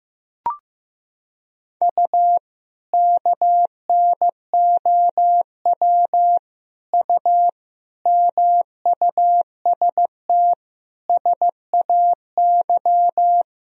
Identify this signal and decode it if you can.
Morse code, heard as a keyed tone, about 15 wpm, 700 Hz. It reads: U KNOW U MUST SAY
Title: Sentences from Top 500 Words and Abbreviations 15wpm